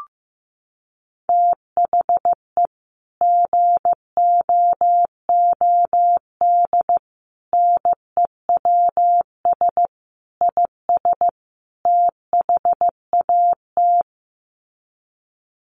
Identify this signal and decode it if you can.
Morse code, heard as a keyed tone, about 15 wpm, 700 Hz. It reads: THE GOOD NEWS IS THAT
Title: Sets of 5 Words 15wpm